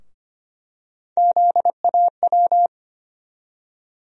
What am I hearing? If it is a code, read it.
Morse code, heard as a keyed tone, about 25 wpm, 700 Hz. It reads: ZAW
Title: USA Call Sign Postfixes - 3 Letters 25wpm